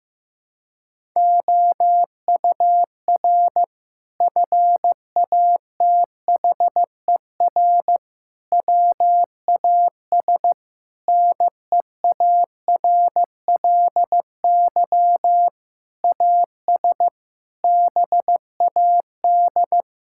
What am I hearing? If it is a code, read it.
Morse code, heard as a keyed tone, about 15 wpm, 700 Hz. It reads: OUR FATHER WAS NEARLY AS BAD
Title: Sentences from Top 1600 Words 15wpm